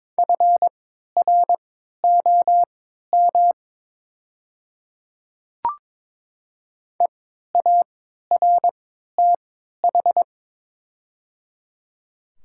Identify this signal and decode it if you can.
Morse code, heard as a keyed tone, about 22 wpm, spaced as for 13 wpm, 700 Hz. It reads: FROM EARTH
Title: 3x Character Spacing - Top 300 Words - 22wpm